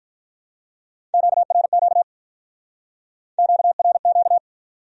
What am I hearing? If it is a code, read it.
Morse code, heard as a keyed tone, about 45 wpm, 700 Hz. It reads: QRQ QRQ